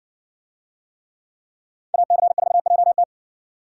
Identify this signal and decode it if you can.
Morse code, heard as a keyed tone, about 60 wpm, 700 Hz. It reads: AC4JT